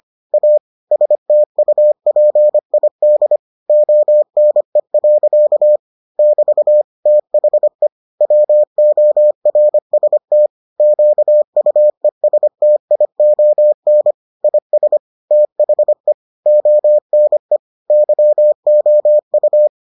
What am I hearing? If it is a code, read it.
Morse code, heard as a keyed tone, about 25 wpm, 600 Hz. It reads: A STUPID ONE. <BT> THE WORST QUESTION IS THE ONE YOU